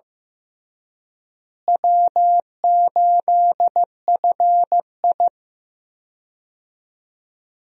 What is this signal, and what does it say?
Morse code, heard as a keyed tone, about 15 wpm, 700 Hz. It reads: W8FI
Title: USA Call Signs - 4 Characters - Repeated 3x - 15wpm